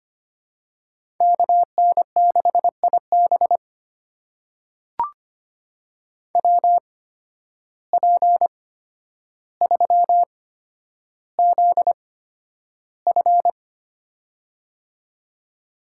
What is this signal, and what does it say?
Morse code, heard as a keyed tone, about 25 wpm, spaced as for 11 wpm, 700 Hz. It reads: KN6IB WP3ZF
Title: USA Call Signs - 5 Characters - 8x Character Spacing - 25wpm